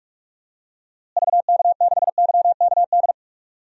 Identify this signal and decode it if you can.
Morse code, heard as a keyed tone, about 46 wpm, 700 Hz. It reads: VK6YKD
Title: DXCC-100 Call Signs - 6 Characters - Repeated 2x - 45wpm